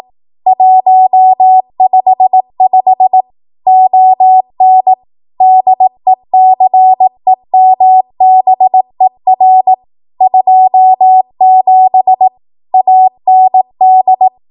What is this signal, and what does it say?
Morse code, heard as a keyed tone, about 18 wpm, 750 Hz. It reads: 155 ON DECEMBER 27 AND
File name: Archive / 18 WPM / 18WPM